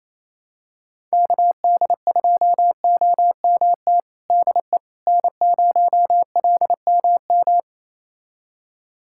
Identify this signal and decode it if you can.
Morse code, heard as a keyed tone, about 27 wpm, 700 Hz. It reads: KD2OMT DE N0LMM
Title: USA Callsign DE Callsign 28wpm